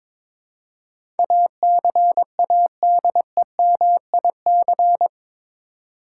Morse code, heard as a keyed tone, about 22 wpm, 700 Hz. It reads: ACADEMIC
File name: practice / 22wpm / Eight Letter Words - 22wpm